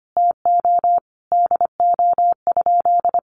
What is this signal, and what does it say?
Morse code, heard as a keyed tone, about 25 wpm, 700 Hz. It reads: TO DO?
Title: Sentences from Top 200 Words 25wpm